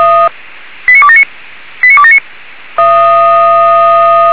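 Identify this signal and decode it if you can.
Five-tone selective call (ZVEI), heard as quick runs of five tones, tones of about 70 ms. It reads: 89279 89279
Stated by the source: ZVEI five-tone call received on SDR, fire brigade Feuerwehr Orlamünde server.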